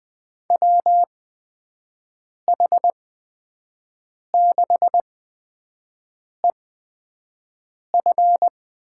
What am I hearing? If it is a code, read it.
Morse code, heard as a keyed tone, about 20 wpm, spaced as for 5 wpm, 700 Hz. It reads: WH6EF
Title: USA Call Signs - 6 Characters - 8x Character Spacing - 20wpm